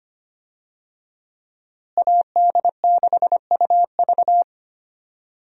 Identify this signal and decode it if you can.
Morse code, heard as a keyed tone, about 25 wpm, 700 Hz. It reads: AD6UV
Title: USA Call Signs - 5 Characters - Repeated 3x - 25wpm